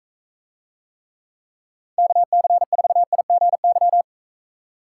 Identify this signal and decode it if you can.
Morse code, heard as a keyed tone, about 42 wpm, 700 Hz. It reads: KC4IGY